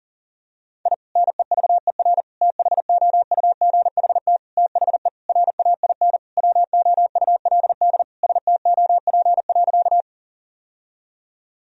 Morse code, heard as a keyed tone, about 40 wpm, 700 Hz. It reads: I NEVER THOUGHT THE RAIN WOULD STOP.
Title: Sentences from Top 1000 Words - Review Part 1 - 40wpm